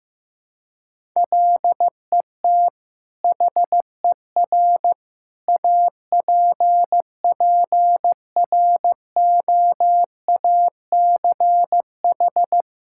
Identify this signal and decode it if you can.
Morse code, heard as a keyed tone, about 15 wpm, 700 Hz. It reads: LET HER APPROACH
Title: Sentences from Top 1000 Words - Review Part 6 - 15wpm